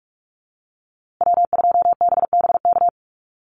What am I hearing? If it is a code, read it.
Morse code, heard as a keyed tone, about 44 wpm, 700 Hz. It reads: W2BBK
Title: USA Call Signs Repeated 2x 45wpm